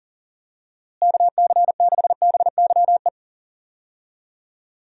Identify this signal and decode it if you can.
Morse code, heard as a keyed tone, about 37 wpm, 700 Hz. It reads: KC6BYE
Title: USA Call Signs - 6 Characters - Repeated 3x - 40wpm